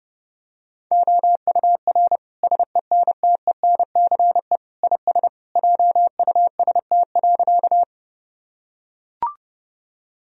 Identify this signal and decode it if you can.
Morse code, heard as a keyed tone, about 32 wpm, 700 Hz. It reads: OUR SENTENCE IS JUST.
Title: Sentences from Top 1000 Words - Review Part 1 - 30wpm